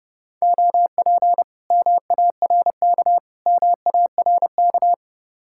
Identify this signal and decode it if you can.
Morse code, heard as a keyed tone, about 29 wpm, 700 Hz. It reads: OP MARK MARK